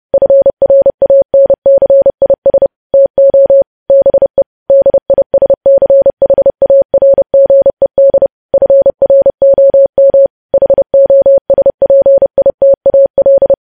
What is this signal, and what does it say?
Morse code, heard as a keyed tone, about 30 wpm, 550 Hz. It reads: FRANCIS TO BE DISCHARGED FROM HOSPITAL
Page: news headlines in morse code at 30 wpm